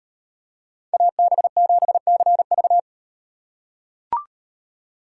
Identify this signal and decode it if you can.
Morse code, heard as a keyed tone, about 38 wpm, 700 Hz. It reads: AB7CV